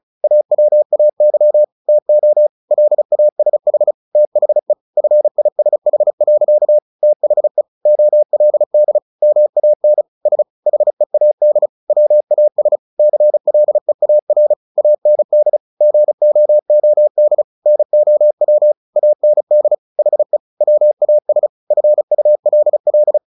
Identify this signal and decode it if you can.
Morse code, heard as a keyed tone, about 34 wpm, 600 Hz. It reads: AWAY TO LASH THE FISH. THE OLD MAN S HEAD WAS CLEAR AND GOOD NOW AND HE WAS FULL
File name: the old man and the sea 2 35wpm